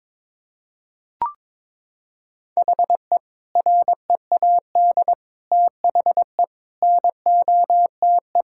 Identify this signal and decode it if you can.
Morse code, heard as a keyed tone, about 22 wpm, 700 Hz. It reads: HE READ THE NOTE
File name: practice / 22wpm / Sentences with 4-Word Phrases - 22wpm